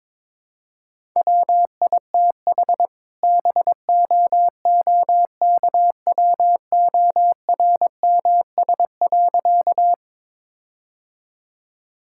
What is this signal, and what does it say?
Morse code, heard as a keyed tone, about 21 wpm, 700 Hz. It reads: WITH BOOKWORMS.
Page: morse-code-ninja-practice-22wpm